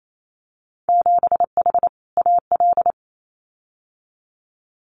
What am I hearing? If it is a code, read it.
Morse code, heard as a keyed tone, about 28 wpm, 700 Hz. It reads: 7H AL
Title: Winter Field Day Exchanges Encore 28wpm